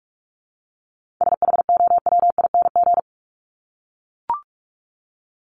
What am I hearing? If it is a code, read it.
Morse code, heard as a keyed tone, about 44 wpm, 700 Hz. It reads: SHOWING
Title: ING Suffix - Top 100 Words with Base Words 45wpm